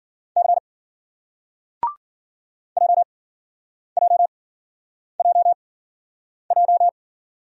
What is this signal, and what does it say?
Morse code, heard as a keyed tone, about 50 wpm, spaced as for 9 wpm, 700 Hz. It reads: C JJJJ
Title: Speed-Racing Single Letters 40wpm